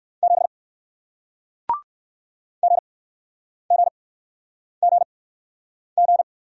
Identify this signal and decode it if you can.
Morse code, heard as a keyed tone, about 56 wpm, spaced as for 9 wpm, 700 Hz. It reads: 7 GGGG
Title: Speed-Racing Single Letter-Number 45wpm